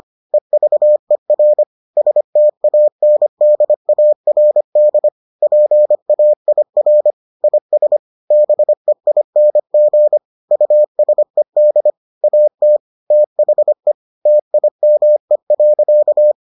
Morse code, heard as a keyed tone, about 25 wpm, 600 Hz. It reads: EVER STANDARD PAIR IS BEING USED AT THE TIME.